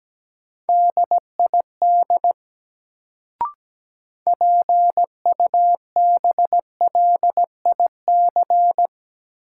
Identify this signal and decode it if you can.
Morse code, heard as a keyed tone, about 17 wpm, 700 Hz. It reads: DID PUBLIC
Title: Sets of 2 Words - New Word Drill - 17wpm